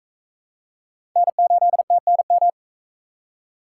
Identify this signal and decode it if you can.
Morse code, heard as a keyed tone, about 40 wpm, 700 Hz. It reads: N8TNM